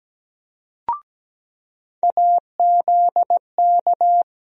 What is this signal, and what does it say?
Morse code, heard as a keyed tone, about 17 wpm, 700 Hz. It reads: AZK
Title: USA Call Sign Postfixes - 3 Letters 17wpm